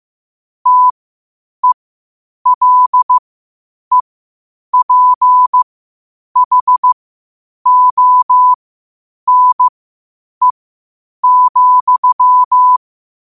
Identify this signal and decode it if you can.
Morse code, heard as a keyed tone, about 15 wpm, spaced as for 8 wpm, 1000 Hz. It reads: TELEPHONE,
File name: qotdfiles / 8wpm